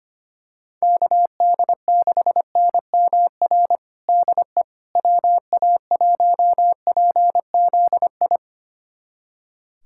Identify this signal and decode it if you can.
Morse code, heard as a keyed tone, about 25 wpm, 700 Hz. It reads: KD6NMR DE WA1PZI
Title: USA Callsign DE Callsign 25wpm